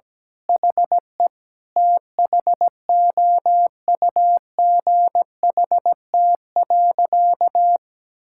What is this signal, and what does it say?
Morse code, heard as a keyed tone, about 17 wpm, 700 Hz. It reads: HE THOUGHT.
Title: Sentences from Top 1600 Words 17wpm